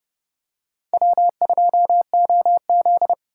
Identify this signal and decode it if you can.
Morse code, heard as a keyed tone, about 30 wpm, 700 Hz. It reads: W2OZ